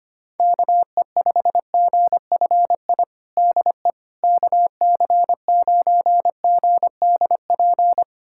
Morse code, heard as a keyed tone, about 25 wpm, 700 Hz. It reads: KE5GFI DE KC9GDP
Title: USA Callsign DE Callsign 25wpm